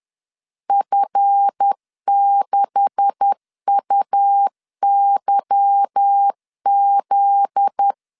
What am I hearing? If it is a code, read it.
Morse code, heard as a keyed tone, about 10 wpm, 800 Hz. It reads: F6UYZ
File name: CW / DET / 1 9 1500C S A 10wpm 4511